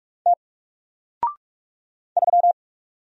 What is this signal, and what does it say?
Morse code, heard as a keyed tone, about 44 wpm, 700 Hz. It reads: T 3
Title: Single Character (Letter:Number:Punctuation:Prosign) Alt 45wpm